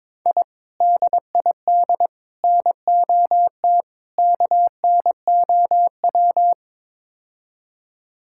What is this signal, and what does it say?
Morse code, heard as a keyed tone, about 22 wpm, 700 Hz. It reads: I DID NOT KNOW
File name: practice / 22wpm / Build A Phrase - 5-Word Phrases - 22wpm